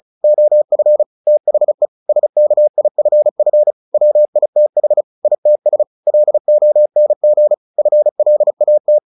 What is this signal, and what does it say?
Morse code, heard as a keyed tone, about 34 wpm, 600 Hz. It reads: OF THE SKIFF WITH ITS LONG FLAT
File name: the old man and the sea 2 35wpm